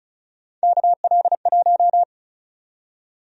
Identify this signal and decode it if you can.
Morse code, heard as a keyed tone, about 34 wpm, 700 Hz. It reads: KL1
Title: USA Call Sign Prefixes - 35wpm